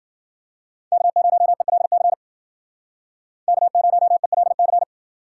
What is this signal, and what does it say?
Morse code, heard as a keyed tone, about 52 wpm, 700 Hz. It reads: K0ELC K0ELC